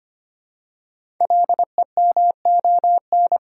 Morse code, heard as a keyed tone, about 25 wpm, 700 Hz. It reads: LEMON
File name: 5 Letter Words - First 3 Letters Given - 25wpm